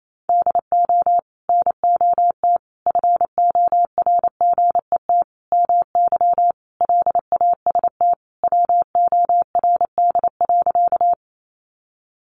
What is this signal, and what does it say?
Morse code, heard as a keyed tone, about 28 wpm, 700 Hz. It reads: DO NOT FORGET MY LAST WORD.
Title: Sentences from Top 1200 Words 28wpm